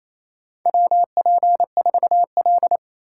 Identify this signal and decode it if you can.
Morse code, heard as a keyed tone, about 28 wpm, 700 Hz. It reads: WP4L
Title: USA Call Signs - 4 Characters - 6x Character Spacing - 28wpm